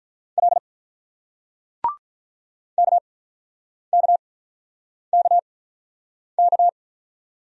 Morse code, heard as a keyed tone, about 46 wpm, spaced as for 9 wpm, 700 Hz. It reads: L KKKK